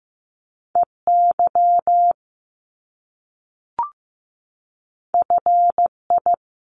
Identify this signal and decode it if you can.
Morse code, heard as a keyed tone, about 15 wpm, 700 Hz. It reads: EY FI
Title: Top Two Letter Combinations in Words 15wpm